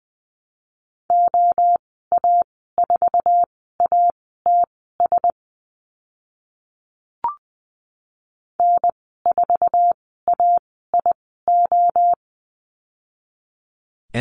Morse code, heard as a keyed tone, about 20 wpm, spaced as for 14 wpm, 700 Hz. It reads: OA4ATS N4AIO
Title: 062 Lesson 11g - Letter h - Call Signs 20wpm